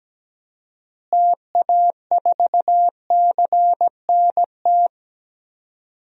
Morse code, heard as a keyed tone, about 17 wpm, 700 Hz. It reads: TA4CNT
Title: DXCC-100 Call Signs - 6 Characters - Repeated 2x - 17wpm